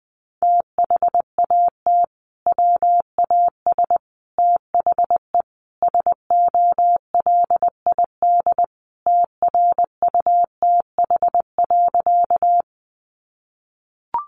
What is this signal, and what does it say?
Morse code, heard as a keyed tone, about 20 wpm, 700 Hz. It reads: THAT WAS THE SOLID TRUTH.
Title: Sentences from Top 1500 Words 20wpm